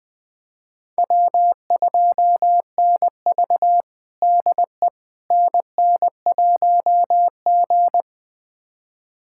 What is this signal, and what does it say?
Morse code, heard as a keyed tone, about 20 wpm, 700 Hz. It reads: W2NV DE NN1G
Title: USA Callsign DE Callsign - 4 Characters - 20wpm